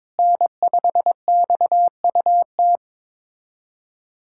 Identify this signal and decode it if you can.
Morse code, heard as a keyed tone, about 21 wpm, 700 Hz. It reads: N5XUT
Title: USA Callsign DE Callsign - 5 Characters - 22wpm